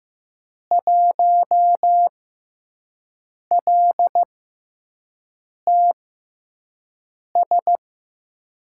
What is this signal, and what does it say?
Morse code, heard as a keyed tone, about 15 wpm, spaced as for 5 wpm, 700 Hz. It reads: 1LTS
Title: USA Call Signs - 6x Character Spacing - 15wpm